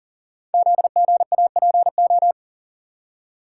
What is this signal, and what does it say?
Morse code, heard as a keyed tone, about 42 wpm, 700 Hz. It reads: ZGAPO
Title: Code Groups - 5 Letters - 40wpm